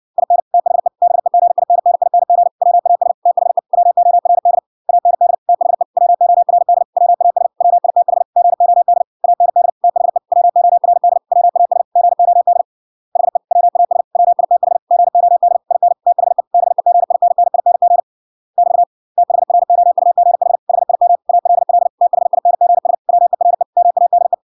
Morse code, heard as a keyed tone, about 56 wpm, 700 Hz. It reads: IN THE BEGINNING WAS THE WORD AND THE WORD WAS WITH GOD AND THE WORD WAS GOD HE WAS WITH GOD IN THE BEGINNING <BT> THROUGH HIM ALL THINGS WERE MADE